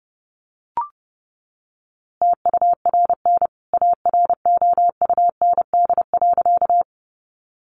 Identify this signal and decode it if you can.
Morse code, heard as a keyed tone, about 30 wpm, 700 Hz. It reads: TURN AROUND.
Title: A Sound Of Thunder 30wpm